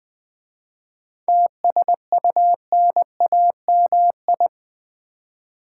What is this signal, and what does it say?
Morse code, heard as a keyed tone, about 20 wpm, 700 Hz. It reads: TSUNAMI
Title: 7 Letter Words - First 3 Letters Given - 20wpm